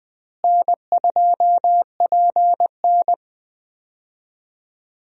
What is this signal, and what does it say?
Morse code, heard as a keyed tone, about 20 wpm, 700 Hz. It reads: N2PN
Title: USA Call Signs - 4 Characters - Repeated 3x - 20wpm